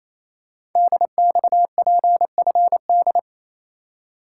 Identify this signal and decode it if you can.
Morse code, heard as a keyed tone, about 29 wpm, 700 Hz. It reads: DXPFD
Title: Code Groups - 5 Letters - 28wpm